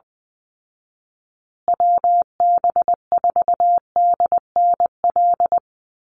Morse code, heard as a keyed tone, about 20 wpm, 700 Hz. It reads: WB4DNL
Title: Call Signs Repeated 2x - 20wpm